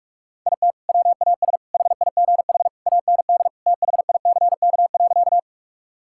Morse code, heard as a keyed tone, about 46 wpm, 700 Hz. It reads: IT WAS HIGH AND THICK.
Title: Sentences from Top 1600 Words 45wpm